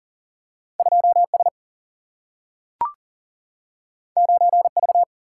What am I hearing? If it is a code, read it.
Morse code, heard as a keyed tone, about 40 wpm, 700 Hz. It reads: 2S 9V
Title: Single-Letter Single-Number 40wpm